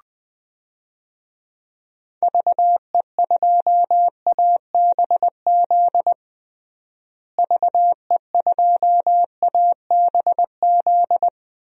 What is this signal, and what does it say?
Morse code, heard as a keyed tone, about 21 wpm, 700 Hz. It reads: VE2ABZ VE2ABZ
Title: DXCC-100 Call Signs - 6 Characters - Repeated 2x - 20wpm